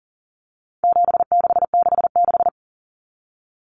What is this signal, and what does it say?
Morse code, heard as a keyed tone, about 40 wpm, 700 Hz. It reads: 7666
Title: Numbers - 4-Digit 40wpm